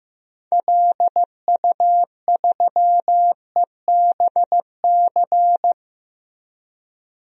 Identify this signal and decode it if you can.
Morse code, heard as a keyed tone, about 15 wpm, 700 Hz. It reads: LU3EBC
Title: DXCC-100 Call Signs - 6 Characters - Repeated 3x - 15wpm